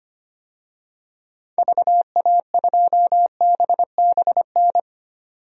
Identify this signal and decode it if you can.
Morse code, heard as a keyed tone, about 25 wpm, 700 Hz. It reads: VA2BBN
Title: DXCC-100 Call Signs - 6 Characters - 25wpm